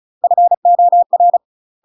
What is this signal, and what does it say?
Morse code, heard as a keyed tone, about 34 wpm, 700 Hz. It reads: FOR
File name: UK MASTER QSOs TRAINING / 35WPM / UK MASTER QSOs 0021